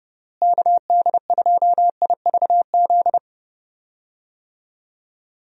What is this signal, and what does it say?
Morse code, heard as a keyed tone, about 30 wpm, 700 Hz. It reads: KD2IVZ
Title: USA Call Signs - 6 Characters - Repeated 2x - 30wpm